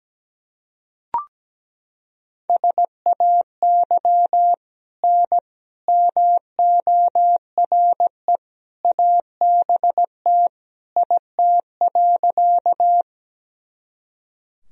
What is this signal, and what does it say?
Morse code, heard as a keyed tone, about 17 wpm, 700 Hz. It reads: SAY N MORE ABT IT.
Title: Sentences from Top 200 Words and Abbreviations 17wpm